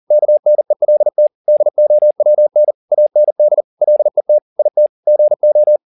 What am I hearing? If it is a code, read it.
Morse code, heard as a keyed tone, about 40 wpm, 600 Hz. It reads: KNELT DOWN AND LET IT GO
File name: the old man and the sea 2 40wpm